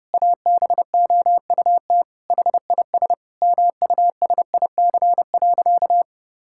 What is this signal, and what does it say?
Morse code, heard as a keyed tone, about 31 wpm, 700 Hz. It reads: ABOUT HIS MUSIC.
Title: Sentences from Top 1000 Words - Review Part 6 - 30wpm